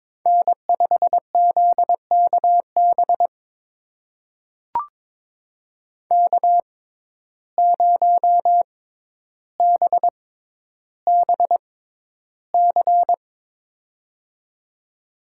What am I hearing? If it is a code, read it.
Morse code, heard as a keyed tone, about 22 wpm, spaced as for 11 wpm, 700 Hz. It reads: N5ZKB K0BBC
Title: USA Call Signs - 5 Characters - 6x Character Spacing - 22wpm